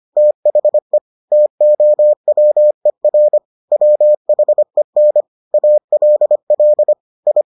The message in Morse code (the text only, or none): THE TOWER WHEN ALL I